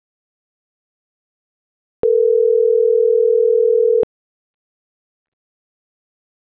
Ringback tone (North America), 440 + 480 Hz.
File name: CENTRAL MAERSK / out alan.vicente